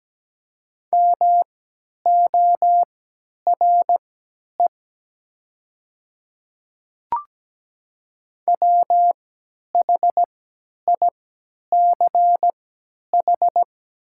Morse code, heard as a keyed tone, about 17 wpm, spaced as for 10 wpm, 700 Hz. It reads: MORE WHICH
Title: 3x Character Spacing - Top 200 Words - 17wpm